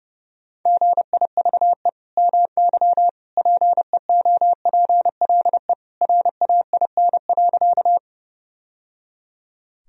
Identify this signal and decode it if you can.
Morse code, heard as a keyed tone, about 30 wpm, 700 Hz. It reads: GIVE MY PEOPLE RAIN.
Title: Sentences from Top 1000 Words - Review Part 1 - 30wpm